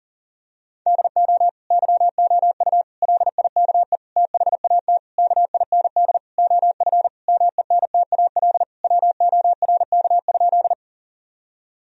Morse code, heard as a keyed tone, about 40 wpm, 700 Hz. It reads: DO YOU LIKE THAT KIND OF MENTAL WORK?
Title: Sentences from Top 1000 Words - Review Part 4 - 40wpm